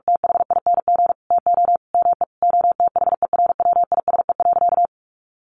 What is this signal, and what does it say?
Morse code, heard as a keyed tone, about 44 wpm, 700 Hz. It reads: THING TO ME OTHERWISE.
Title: Sentences from Top 1200 Words 45wpm